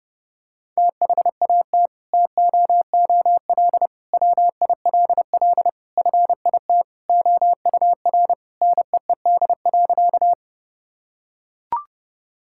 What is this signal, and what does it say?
Morse code, heard as a keyed tone, about 30 wpm, 700 Hz. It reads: THAT TOOL WILL FIT OUR NEED.
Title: Sentences from Top 1000 Words - Review Part 6 - 30wpm